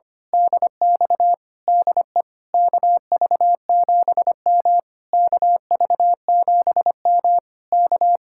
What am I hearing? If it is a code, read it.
Morse code, heard as a keyed tone, about 24 wpm, 700 Hz. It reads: DX DE KV7M KV7M K
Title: Speed-Racing Common QSO Phrases - 25wpm